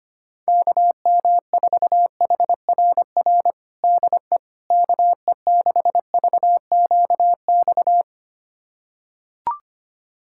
Morse code, heard as a keyed tone, about 25 wpm, 700 Hz. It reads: KM4HRR DE KE6VQX
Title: USA Callsign DE Callsign 25wpm